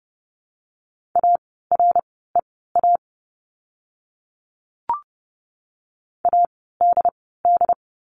Morse code, heard as a keyed tone, about 30 wpm, spaced as for 17 wpm, 700 Hz. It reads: AREA ADD
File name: Lesson 17f - Letter u - Words 30wpm